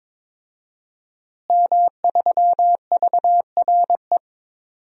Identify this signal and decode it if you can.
Morse code, heard as a keyed tone, about 22 wpm, 700 Hz. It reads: M3VRE
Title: DXCC-100 Call Signs - 5 Characters - Repeated 3x - 22wpm